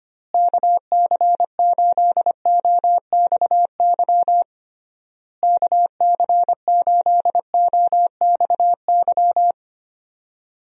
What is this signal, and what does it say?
Morse code, heard as a keyed tone, about 25 wpm, 700 Hz. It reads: KC8OXY KC8OXY
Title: USA Call Signs - 6 Characters - Repeated 3x - 25wpm